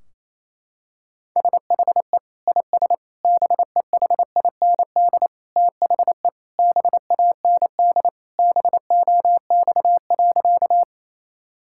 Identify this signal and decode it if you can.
Morse code, heard as a keyed tone, about 28 wpm, 700 Hz. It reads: SHE IS BEHIND THE BAND BOX.
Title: Sentences from Top 1600 Words 28wpm